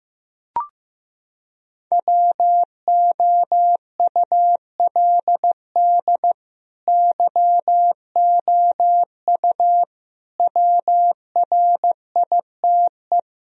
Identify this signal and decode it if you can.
Morse code, heard as a keyed tone, about 15 wpm, 700 Hz. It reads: WOULD YOU WRITE
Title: Sentences from Top 1000 Words - Review Part 2 - 15wpm